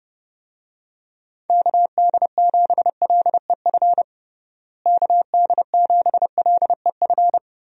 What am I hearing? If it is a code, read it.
Morse code, heard as a keyed tone, about 32 wpm, 700 Hz. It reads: KD7LEF KD7LEF
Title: USA Call Signs - 6 Characters - Repeated 2x - 30wpm